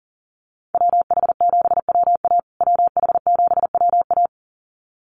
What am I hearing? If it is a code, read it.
Morse code, heard as a keyed tone, about 40 wpm, 700 Hz. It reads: WH7WA WH7WA
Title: USA Call Signs Repeated 2x 40wpm